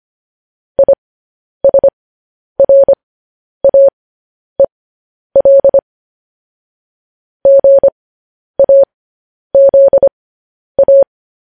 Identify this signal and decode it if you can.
Morse code, heard as a keyed tone, about 25 wpm, spaced as for 10 wpm, 550 Hz. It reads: ISRAEL GAZA